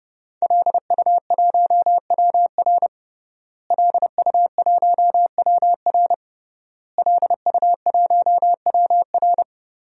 Morse code, heard as a keyed tone, about 30 wpm, 700 Hz. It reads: LU1WR LU1WR LU1WR
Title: DXCC-100 Call Signs - 5 Characters - Repeated 3x - 30wpm